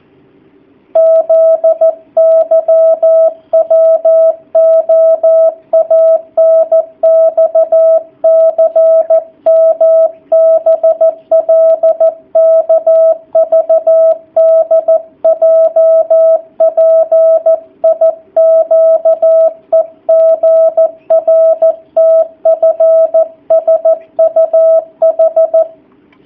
Morse code, heard as a keyed tone, about 14 wpm, 650 Hz. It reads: ZYWOANXCMBLKVDJPIQEGRTFSUH
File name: jh8cbh / 2019cw / sp13- w